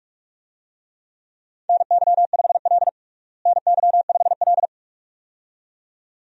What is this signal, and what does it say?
Morse code, heard as a keyed tone, about 46 wpm, 700 Hz. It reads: NY5L NY5L
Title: USA Call Signs - 4 Characters - Repeated 2x - 45wpm